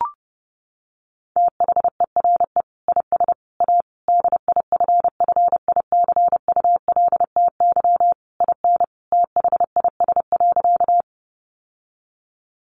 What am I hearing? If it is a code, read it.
Morse code, heard as a keyed tone, about 30 wpm, 700 Hz. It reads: THERE IS A DIFFICULTY IN THIS.
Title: Sentences from Top 1500 Words 30wpm